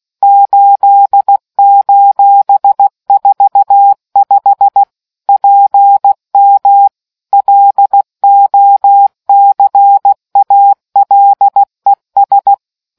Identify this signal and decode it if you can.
Morse code, heard as a keyed tone, about 16 wpm, 800 Hz. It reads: 8:45 PM LOCALES